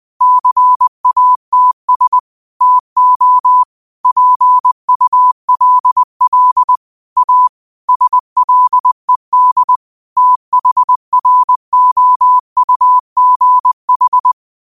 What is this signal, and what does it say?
Morse code, heard as a keyed tone, about 20 wpm, 1000 Hz. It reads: CATS TO PULL A SLED THROUGH